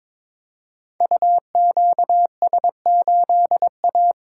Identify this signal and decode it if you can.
Morse code, heard as a keyed tone, about 23 wpm, 700 Hz. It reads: UQS8A
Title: Code Groups - 5 Letter:Numbers - 22wpm